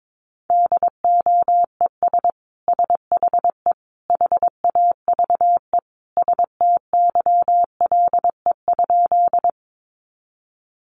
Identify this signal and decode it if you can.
Morse code, heard as a keyed tone, about 22 wpm, 700 Hz. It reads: DOES SHE HAVE STYLE?
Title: Sentences from Top 800 Words 22wpm